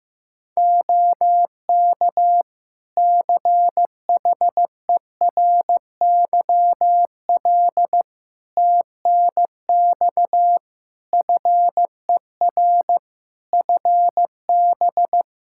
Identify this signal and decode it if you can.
Morse code, heard as a keyed tone, about 16 wpm, 700 Hz. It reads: OK CHERYL TNX FER FB
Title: Practice QSO - Exchange 4 - Ending - 15wpm